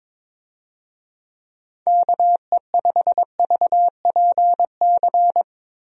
Morse code, heard as a keyed tone, about 22 wpm, 700 Hz. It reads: KE5VPC
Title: USA Call Signs - 6 Characters - Repeated 3x - 22wpm